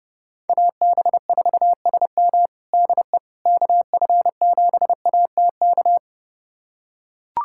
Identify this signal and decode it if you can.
Morse code, heard as a keyed tone, about 30 wpm, 700 Hz. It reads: AB4SM DE KF7ATK
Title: USA Callsign DE Callsign 30wpm